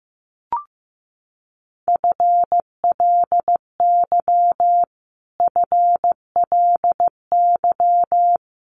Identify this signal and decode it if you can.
Morse code, heard as a keyed tone, about 15 wpm, 700 Hz. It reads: FLY FLY
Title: Top 500 Words Repeated 15wpm